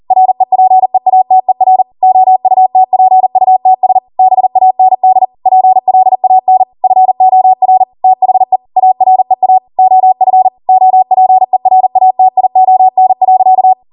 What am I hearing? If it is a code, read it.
Morse code, heard as a keyed tone, about 40 wpm, 750 Hz. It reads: REPEATER OUTPUTS BAND PLAN FOR THE AREA OF OPERATION.
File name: Morse / 40WPM